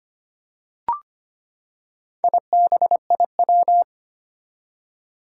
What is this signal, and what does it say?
Morse code, heard as a keyed tone, about 25 wpm, 700 Hz. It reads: IBIW
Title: Four Challenging Letter-Number Combinations 25wpm